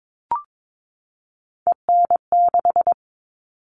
Morse code, heard as a keyed tone, about 22 wpm, 700 Hz. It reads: EN6
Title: All Cut Numbers - 3-Digit 22wpm